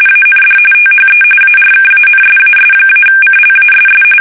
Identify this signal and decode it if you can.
RTTY, about 45.45 baud. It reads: AND THE ARKANSAS-CLASS
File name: fsk850 / usnavy3